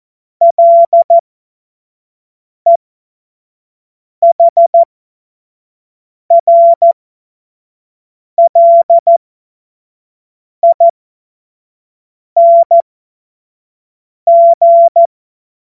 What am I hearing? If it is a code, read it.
Morse code, heard as a keyed tone, about 14 wpm, spaced as for 5 wpm, 680 Hz. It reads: LEHRLING
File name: Morsekurs / Witze / Witz 25. Oktober 2022 25 F70